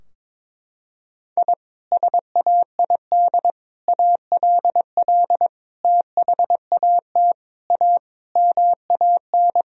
Morse code, heard as a keyed tone, about 22 wpm, 700 Hz. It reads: I SAID ALL THAT A MAN
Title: Sentences from Top 1000 Words - Review Part 2 - 22wpm